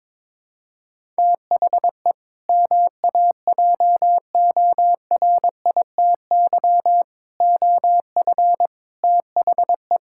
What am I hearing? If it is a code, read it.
Morse code, heard as a keyed tone, about 22 wpm, 700 Hz. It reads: THE MAJORITY OF THE
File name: practice / 22wpm / Sentences with 4-Word Phrases - 22wpm